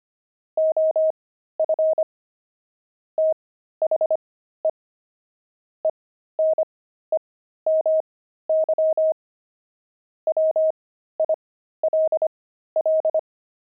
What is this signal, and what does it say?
Morse code, heard as a keyed tone, about 25 wpm, spaced as for 13 wpm, 630 Hz. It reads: OF THE ENEMY WILL
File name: pangram-fs-13-cs-25-enemy